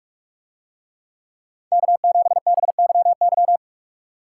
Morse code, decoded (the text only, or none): K7BYY